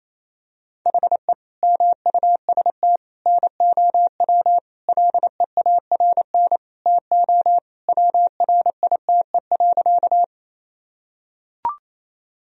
Morse code, heard as a keyed tone, about 27 wpm, 700 Hz. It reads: HE MUST NOW LEARN TO WRITE.
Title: Sentences from Top 1000 Words - Review Part 1 - 28wpm